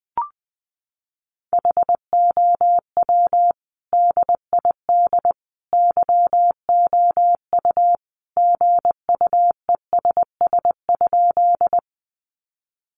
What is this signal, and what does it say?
Morse code, heard as a keyed tone, about 20 wpm, 700 Hz. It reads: HOW DID YOU GUESS?
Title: Sentences from Top 1500 Words 20wpm